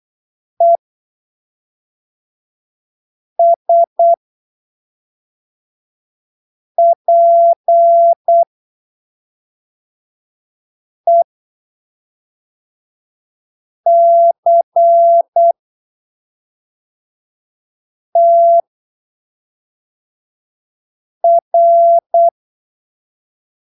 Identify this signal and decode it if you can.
Morse code, heard as a keyed tone, about 8 wpm, spaced as for 3 wpm, 680 Hz. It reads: ESPECTR
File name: arquivos / texto / texto 12 08ppm